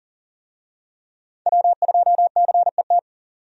Morse code, heard as a keyed tone, about 38 wpm, 700 Hz. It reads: W2CET
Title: USA Call Signs - 5 Characters - 40wpm